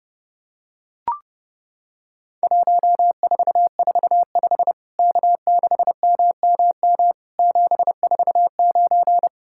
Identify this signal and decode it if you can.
Morse code, heard as a keyed tone, about 32 wpm, 700 Hz. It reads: 1445 K6MMM 749
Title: Worked All Europe DX Contest - QTC Exchange - 30wpm